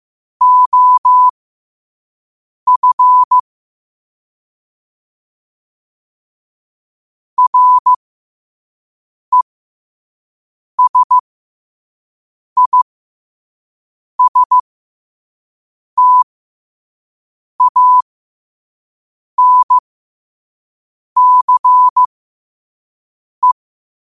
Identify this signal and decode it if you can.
Morse code, heard as a keyed tone, about 15 wpm, spaced as for 5 wpm, 1000 Hz. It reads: OF RESISTANCE
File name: qotdfiles / 5wpm